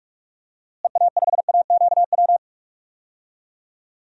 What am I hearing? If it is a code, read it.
Morse code, heard as a keyed tone, about 41 wpm, 700 Hz. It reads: EA5AQW
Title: DXCC-100 Call Signs - 6 Characters - Repeated 3x - 45wpm